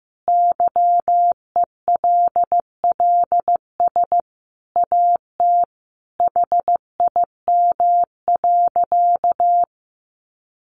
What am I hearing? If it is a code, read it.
Morse code, heard as a keyed tone, about 14 wpm, 700 Hz. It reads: YELLS AT HIM.
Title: Additional Easy Phrases 15wpm